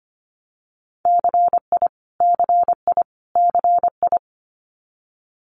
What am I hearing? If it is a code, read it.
Morse code, heard as a keyed tone, about 25 wpm, 700 Hz. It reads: CI CI CI
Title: Multi-letter Phonemes Repeated 3x 25wpm